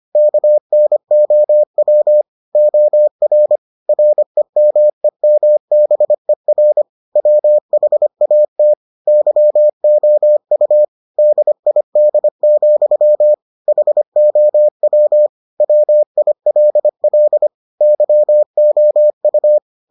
Morse code, heard as a keyed tone, about 25 wpm, 600 Hz. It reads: KNOW OR REMEMBER WHAT YOU DID, HOW WILL YOU